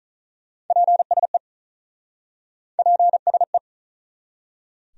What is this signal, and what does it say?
Morse code, heard as a keyed tone, about 48 wpm, 700 Hz. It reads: PSE PSE